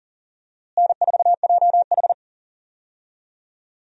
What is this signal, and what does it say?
Morse code, heard as a keyed tone, about 40 wpm, 700 Hz. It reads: N4JH
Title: USA Call Signs - 4 Characters - Repeated 3x - 40wpm